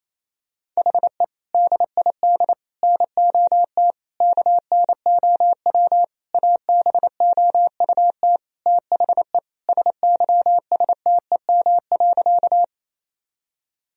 Morse code, heard as a keyed tone, about 27 wpm, 700 Hz. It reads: HE DID NOT KNOW ABOUT THE SYSTEM.
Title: Sentences from Top 1000 Words - Review Part 3 - 28wpm